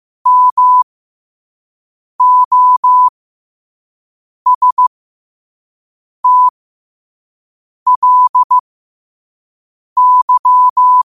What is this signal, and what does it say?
Morse code, heard as a keyed tone, about 15 wpm, spaced as for 5 wpm, 1000 Hz. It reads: MOSTLY
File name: qotdfiles / 5wpm